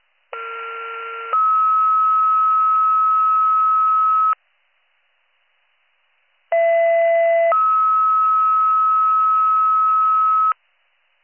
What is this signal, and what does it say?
Two-tone page, heard as a one-second tone, then a three-second tone, about 1370 then 1290 Hz, about 670 then 1290 Hz. It is heard from 0.3 to 10.5 s.